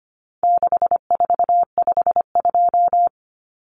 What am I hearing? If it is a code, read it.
Morse code, heard as a keyed tone, about 25 wpm, 700 Hz. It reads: 6452
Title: Common Cut Numbers - 4-Digit 25wpm